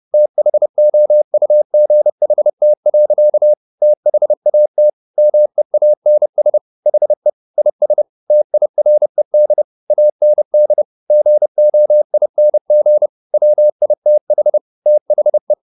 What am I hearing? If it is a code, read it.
Morse code, heard as a keyed tone, about 30 wpm, 600 Hz. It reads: THOUGHT. THAT MEANS HE IS TIRED AND GOING WITH THE